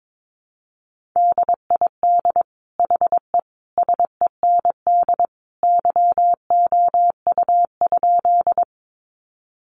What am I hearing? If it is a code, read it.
Morse code, heard as a keyed tone, about 22 wpm, 700 Hz. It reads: DID HE SEND YOU?
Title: Sentences from Top 1500 Words 22wpm